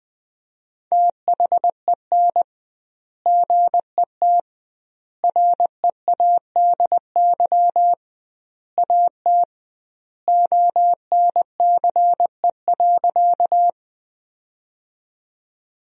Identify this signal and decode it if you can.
Morse code, heard as a keyed tone, about 21 wpm, 700 Hz. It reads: THEN GET READY AT ONCE.
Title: 2x Word-Spacing - Sentences from Top 400 Words - 20wpm